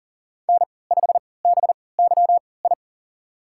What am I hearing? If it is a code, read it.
Morse code, heard as a keyed tone, about 40 wpm, spaced as for 23 wpm, 700 Hz. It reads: N5BYI
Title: USA Call Signs - 3x Character Spacing - 40wpm